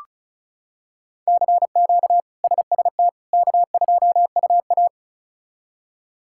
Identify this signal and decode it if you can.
Morse code, heard as a keyed tone, about 34 wpm, 700 Hz. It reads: CQ SST K2UA
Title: K1USN SST QSO Practice - 35wpm